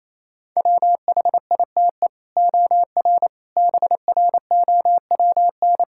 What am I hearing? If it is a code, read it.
Morse code, heard as a keyed tone, about 28 wpm, 700 Hz. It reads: WHITE OR BROWN
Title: Sentences from Top 1600 Words 28wpm